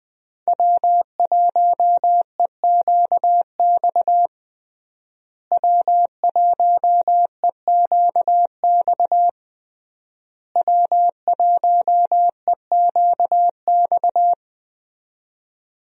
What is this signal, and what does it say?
Morse code, heard as a keyed tone, about 20 wpm, 700 Hz. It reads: W1EQX W1EQX W1EQX
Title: USA Call Signs - 5 Characters - Repeated 3x - 20wpm